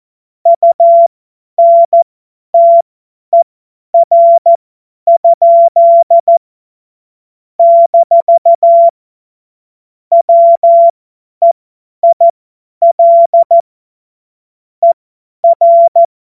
Morse code, heard as a keyed tone, about 14 wpm, spaced as for 10 wpm, 680 Hz. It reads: UNTER? - WEIL ER
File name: Morsekurs / Witze / Witz 16. November 2021, 50 F70